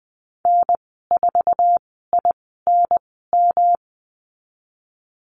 Morse code, heard as a keyed tone, about 20 wpm, spaced as for 15 wpm, 700 Hz. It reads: N4INM
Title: 186 Lesson 30g - Letter b - Call Signs 20wpm